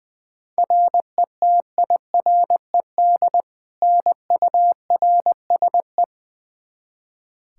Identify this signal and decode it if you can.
Morse code, heard as a keyed tone, about 20 wpm, 700 Hz. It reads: RETIRED NURSE
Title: Practice QSO - Exchange 3 - More Info - 20wpm